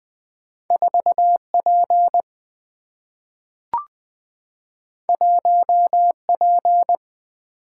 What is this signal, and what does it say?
Morse code, heard as a keyed tone, about 20 wpm, 700 Hz. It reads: 4 P 1 P
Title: Single-Letter Single-Number 20wpm